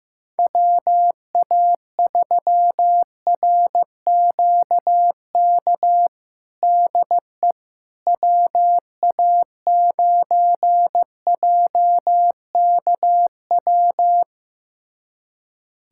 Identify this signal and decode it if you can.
Morse code, heard as a keyed tone, about 15 wpm, 700 Hz. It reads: WA3RQK DE WA9JKW
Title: USA Callsign DE Callsign - 6 Characters - 15wpm